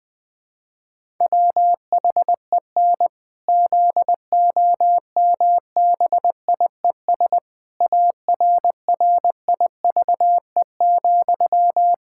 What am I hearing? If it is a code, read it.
Morse code, heard as a keyed tone, about 20 wpm, 700 Hz. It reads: WHEN ZOMBIES ARRIVE,
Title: Pangram Challenge v2 - 20wpm